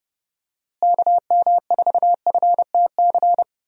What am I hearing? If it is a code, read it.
Morse code, heard as a keyed tone, about 31 wpm, 700 Hz. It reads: KM4FTC
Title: USA Call Signs - 6 Characters - Repeated 3x - 30wpm